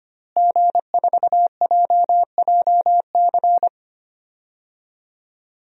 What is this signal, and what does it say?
Morse code, heard as a keyed tone, about 25 wpm, 700 Hz. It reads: G4JJC
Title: DXCC-100 Call Signs - 5 Characters - Repeated 2x - 25wpm